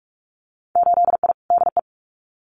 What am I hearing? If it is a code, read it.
Morse code, heard as a keyed tone, about 44 wpm, 700 Hz. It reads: 8I DE
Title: Winter Field Day Exchanges Encore 45wpm